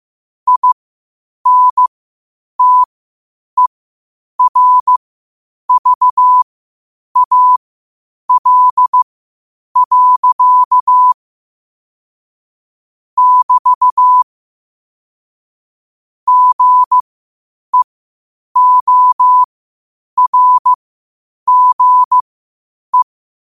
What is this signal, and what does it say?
Morse code, heard as a keyed tone, about 15 wpm, spaced as for 8 wpm, 1000 Hz. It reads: INTERVAL. <BT> GEORGE
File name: qotdfiles / 8wpm